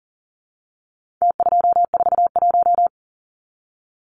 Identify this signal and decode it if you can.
Morse code, heard as a keyed tone, about 40 wpm, 700 Hz. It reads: T241
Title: Numbers - Contest Style - 4-Digit 40wpm